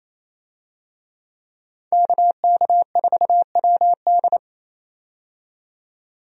Morse code, heard as a keyed tone, about 27 wpm, 700 Hz. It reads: KK4WD
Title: USA Call Signs - 5 Characters - 28wpm